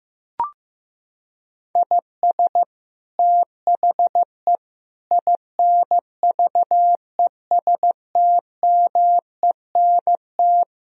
Morse code, heard as a keyed tone, about 15 wpm, 700 Hz. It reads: IS THE INVESTMENT
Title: Sentences from Top 1000 Words - Review Part 4 - 15wpm